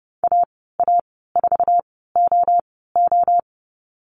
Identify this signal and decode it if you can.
Morse code, heard as a keyed tone, about 32 wpm, spaced as for 17 wpm, 700 Hz. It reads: AA4OO
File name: Lesson 15g - Number 5 - Call Signs 30wpm